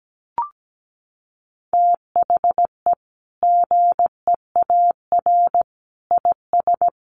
Morse code, heard as a keyed tone, about 17 wpm, 700 Hz. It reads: THE GEAR IS
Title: Sentences from Top 1400 Words 17wpm